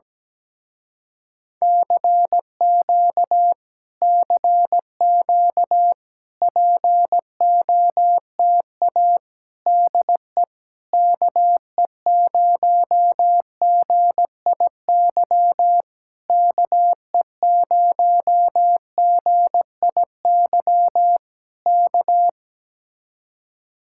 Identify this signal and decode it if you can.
Morse code, heard as a keyed tone, about 17 wpm, 700 Hz. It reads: CQ CQ POTA DE KE0GIY KE0GIY K
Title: Speed-Racing Common QSO Phrases - 17wpm